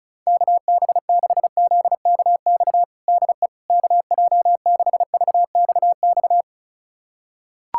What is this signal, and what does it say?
Morse code, heard as a keyed tone, about 36 wpm, 700 Hz. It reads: KB6ZKX DE KJ6VXX
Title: USA Callsign DE Callsign - 6 Characters - 35wpm